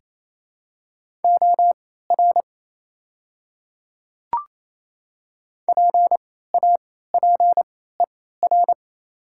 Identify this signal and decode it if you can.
Morse code, heard as a keyed tone, about 28 wpm, spaced as for 16 wpm, 700 Hz. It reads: OR PAPER
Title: 3x Character Spacing - Top 300 Words - 28wpm